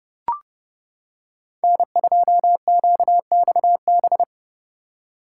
Morse code, heard as a keyed tone, about 30 wpm, 700 Hz. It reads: N2QXB